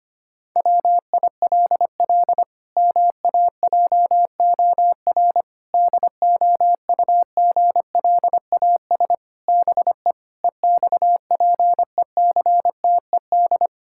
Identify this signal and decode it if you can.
Morse code, heard as a keyed tone, about 25 wpm, 700 Hz. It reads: WILL MAJOR DOUGLAS BE EXPECTED
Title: Pangram Challenge v2 - 25wpm